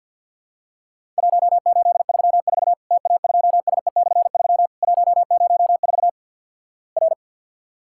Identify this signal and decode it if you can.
Morse code, heard as a keyed tone, about 50 wpm, 700 Hz. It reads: 1834 TA2SE/3 104 R
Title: Worked All Europe DX Contest - QTC Exchange with Repeats - 50wpm